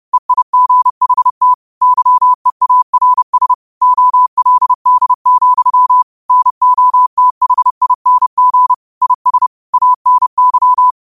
Morse code, heard as a keyed tone, about 30 wpm, 1000 Hz. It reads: EIGHT YEARS OLD, NOTHING IS ANY